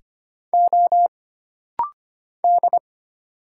Morse code, heard as a keyed tone, about 25 wpm, 700 Hz. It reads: O D
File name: Single Character (Letter:Number:Punctuation:Prosign) - Rapid-Fire - 25wpm